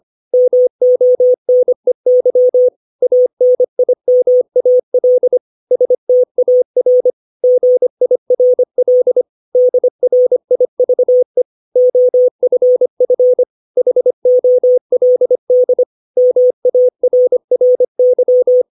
Morse code, heard as a keyed tone, about 25 wpm, 500 Hz. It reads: MONEY ANIMAL STAR GIRL DRIVE OFF HOLD MARRY
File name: 632 words / 632 Words 25WPM